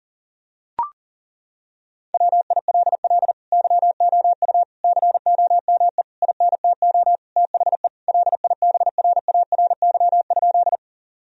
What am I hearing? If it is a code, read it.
Morse code, heard as a keyed tone, about 37 wpm, 700 Hz. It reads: WILL YOU COME INTO THE LIBRARY?
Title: Sentences from Top 1000 Words - Review Part 4 - 40wpm